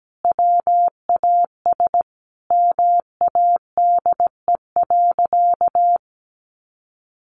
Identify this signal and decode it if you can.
Morse code, heard as a keyed tone, about 17 wpm, 700 Hz. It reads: WAS MADE.
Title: Sentences from Top 1400 Words 17wpm